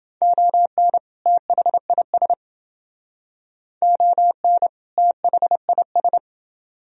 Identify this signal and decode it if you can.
Morse code, heard as a keyed tone, about 28 wpm, 700 Hz. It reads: ON THIS ON THIS